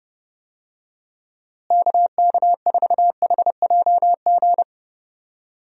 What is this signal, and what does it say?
Morse code, heard as a keyed tone, about 30 wpm, 700 Hz. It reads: KK4HJG
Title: USA Call Signs - 6 Characters - Repeated 3x - 30wpm